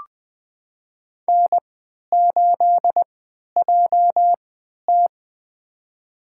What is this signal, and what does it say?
Morse code, heard as a keyed tone, about 20 wpm, spaced as for 11 wpm, 700 Hz. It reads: N8JT
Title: USA Call Signs - 3x Character Spacing - 20wpm